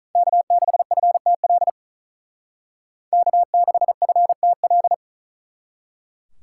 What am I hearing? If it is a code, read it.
Morse code, heard as a keyed tone, about 43 wpm, 700 Hz. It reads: K6FTL K6FTL
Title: Speed Racing - USA 1x3 Call Signs 35wpm